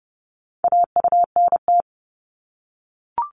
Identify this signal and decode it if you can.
Morse code, heard as a keyed tone, about 30 wpm, 700 Hz. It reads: AUNT
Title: Multi-letter Phonemes 30wpm